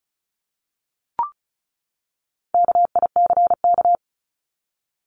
Morse code, heard as a keyed tone, about 34 wpm, 700 Hz. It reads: KICK
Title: Top 1251-1500 Words 35wpm